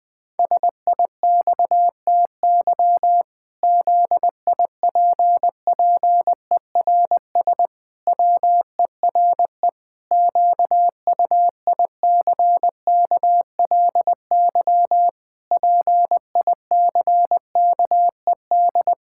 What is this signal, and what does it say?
Morse code, heard as a keyed tone, about 20 wpm, 700 Hz. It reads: SIXTY ZIPPERS WERE QUICKLY PICKED
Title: Pangram Challenge v2 - 20wpm